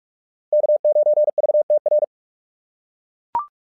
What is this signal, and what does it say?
Morse code, heard as a keyed tone, about 44 wpm, 600 Hz. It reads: K9VTR
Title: SOTA QSO Practice 45wpm